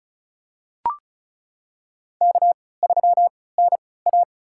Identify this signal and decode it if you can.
Morse code, heard as a keyed tone, about 36 wpm, spaced as for 20 wpm, 700 Hz. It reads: K3NA
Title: USA Call Signs - 4 Characters - 3x Character Spacing - 35wpm